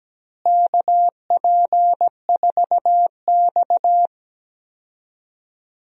Morse code, heard as a keyed tone, about 17 wpm, 700 Hz. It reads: KP4X
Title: DXCC-100 Call Signs - 4 Characters - Repeated 3x - 17wpm